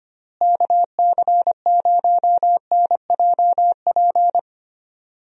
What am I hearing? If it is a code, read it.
Morse code, heard as a keyed tone, about 25 wpm, 700 Hz. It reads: KC0NJP
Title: USA Call Signs - 6 Characters - Repeated 2x - 25wpm